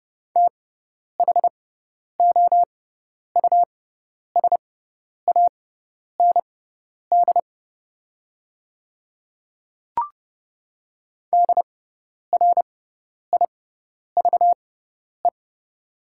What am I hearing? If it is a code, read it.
Morse code, heard as a keyed tone, about 30 wpm, spaced as for 10 wpm, 690 Hz. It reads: THOUSAND DRIVE
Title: 6x Character Spacing - Top 500 Words - 30wpm